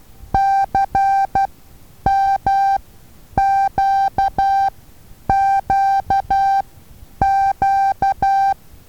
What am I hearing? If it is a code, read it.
Morse code, heard as a keyed tone, about 12 wpm, spaced as for 9 wpm, 780 Hz. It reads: CMQQQ
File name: Taviro HA6VA A